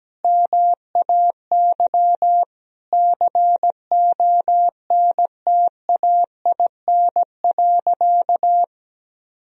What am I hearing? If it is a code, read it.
Morse code, heard as a keyed tone, about 17 wpm, 700 Hz. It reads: MAY CONTAIN.
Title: Sentences from Top 1000 Words - Review Part 2 - 17wpm